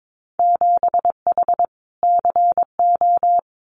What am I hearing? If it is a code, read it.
Morse code, heard as a keyed tone, about 22 wpm, 700 Hz. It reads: 7H CO
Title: Winter Field Day Exchanges Encore 22wpm